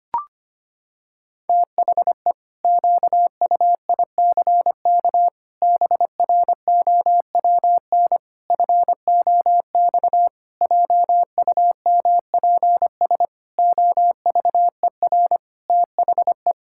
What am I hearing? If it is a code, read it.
Morse code, heard as a keyed tone, about 25 wpm, 700 Hz. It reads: THE QUICK BROWN FOX JUMPS OVER THE
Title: Pangram Challenge v2 - 25wpm